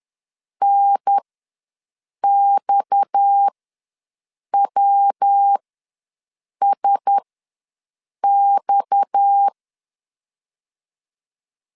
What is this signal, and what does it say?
Morse code, heard as a keyed tone, about 10 wpm, spaced as for 6 wpm, 800 Hz. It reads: NXWSX
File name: LEZIONI / 6 PKX0 / 6 PKX ws 10wpm 3706